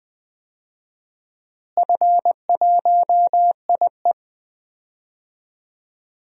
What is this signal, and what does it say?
Morse code, heard as a keyed tone, about 20 wpm, 700 Hz. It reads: F1IE
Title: DXCC-100 Call Signs - 4 Characters - Repeated 3x - 20wpm